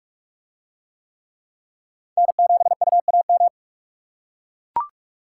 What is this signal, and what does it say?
Morse code, heard as a keyed tone, about 41 wpm, 700 Hz. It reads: N7UAM